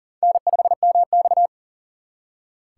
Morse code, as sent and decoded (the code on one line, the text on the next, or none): -. ..... -- -..-
N5MX